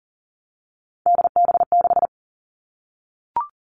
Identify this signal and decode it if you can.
Morse code, heard as a keyed tone, about 42 wpm, 700 Hz. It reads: DB6
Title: Three Challenging Letter-Number Combinations 40wpm